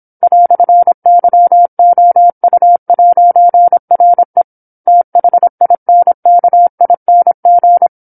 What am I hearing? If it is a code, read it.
Morse code, heard as a keyed tone, about 27 wpm, 700 Hz. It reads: "YOU'RE THINKING